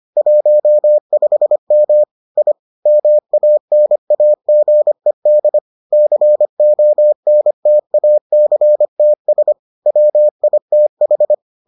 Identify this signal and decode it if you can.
Morse code, heard as a keyed tone, about 25 wpm, 600 Hz. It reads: 15M I MANAGED CONTACTS WITH